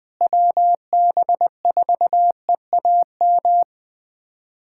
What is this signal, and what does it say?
Morse code, heard as a keyed tone, about 20 wpm, 700 Hz. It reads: WB4EAM
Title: USA Callsign DE Callsign - 6 Characters - 20wpm